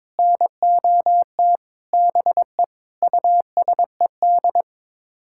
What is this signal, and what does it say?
Morse code, heard as a keyed tone, about 22 wpm, 700 Hz. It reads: NOT BE USED
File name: Sets of 9 Words - 22wpm